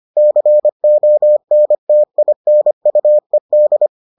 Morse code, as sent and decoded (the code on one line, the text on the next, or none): -.-. --- -. - .. -. ..- . -..
CONTINUED